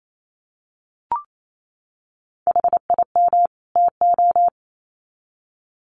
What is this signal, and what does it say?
Morse code, heard as a keyed tone, about 28 wpm, 700 Hz. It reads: HIM TO
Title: Sets of 2 Words 28wpm